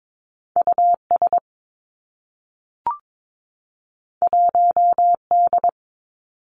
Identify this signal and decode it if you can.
Morse code, heard as a keyed tone, about 22 wpm, 700 Hz. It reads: U S 1 D
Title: Two Challenging Letter-Number Combinations 22wpm